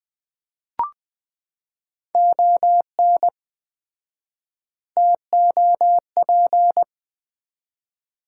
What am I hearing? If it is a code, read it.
Morse code, heard as a keyed tone, about 20 wpm, 700 Hz. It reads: ON TOP
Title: 4x Word-Spacing - Sets of 3 Words - 20wpm